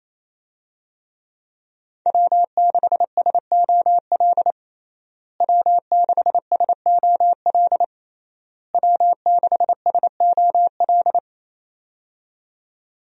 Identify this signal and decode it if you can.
Morse code, heard as a keyed tone, about 28 wpm, 700 Hz. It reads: W6SOL W6SOL W6SOL
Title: USA Call Signs - 5 Characters - Repeated 3x - 28wpm